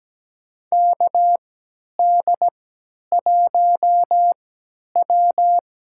Morse code, heard as a keyed tone, about 17 wpm, spaced as for 10 wpm, 700 Hz. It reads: KD1W
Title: USA Call Signs - 5 Characters - 3x Character Spacing - 17wpm